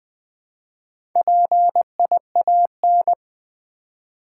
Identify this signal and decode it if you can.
Morse code, heard as a keyed tone, about 20 wpm, 700 Hz. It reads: PIAN
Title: Build A Word - 1 Letter At A Time - 3 to 5-Letter Words - 20wpm